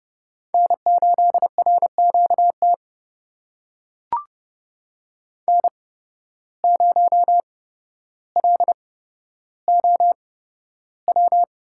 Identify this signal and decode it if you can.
Morse code, heard as a keyed tone, about 29 wpm, spaced as for 13 wpm, 700 Hz. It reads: N8RQT N0LOW
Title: USA Call Signs - 5 Characters - 8x Character Spacing - 30wpm